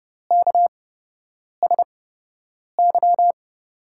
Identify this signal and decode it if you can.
Morse code, heard as a keyed tone, about 30 wpm, spaced as for 8 wpm, 700 Hz. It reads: KSY
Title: USA Call Signs - 8x Character Spacing - 30wpm